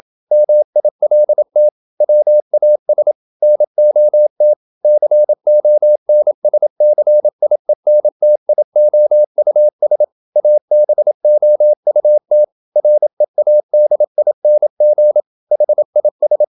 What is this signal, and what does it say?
Morse code, heard as a keyed tone, about 27 wpm, 600 Hz. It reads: MILT WAS NOT CONSCIENTIOUS ABOUT READING HIS
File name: Never 0027wpm